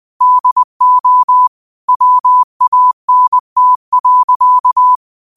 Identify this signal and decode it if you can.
Morse code, heard as a keyed tone, about 20 wpm, 1000 Hz. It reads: DO WANT.